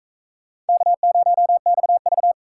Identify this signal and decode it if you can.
Morse code, heard as a keyed tone, about 40 wpm, 700 Hz. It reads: K0XV